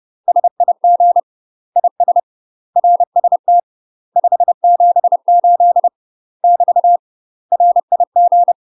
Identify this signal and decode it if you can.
Morse code, heard as a keyed tone, about 30 wpm, 700 Hz. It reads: SIG IS RST 578 <BT> RIG
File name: UK MASTER QSOs TRAINING / 30WPM / UK MASTER QSOs 0008